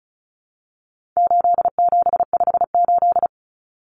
Morse code, heard as a keyed tone, about 34 wpm, 700 Hz. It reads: 8758
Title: Numbers - 4-Digit 35wpm